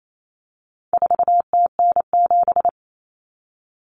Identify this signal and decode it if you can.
Morse code, heard as a keyed tone, about 28 wpm, 700 Hz. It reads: 4TN7
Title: Common Cut Numbers - 4-Digit 28wpm